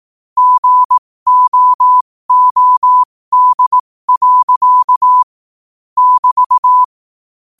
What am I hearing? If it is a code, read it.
Morse code, heard as a keyed tone, about 18 wpm, spaced as for 15 wpm, 1000 Hz. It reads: GOOD. <BT>
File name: qotdfiles / 15wpm